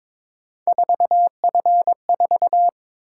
Morse code, heard as a keyed tone, about 22 wpm, 700 Hz. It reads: 4F4
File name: practice / 22wpm / Three Letter-Number Combinations - 22wpm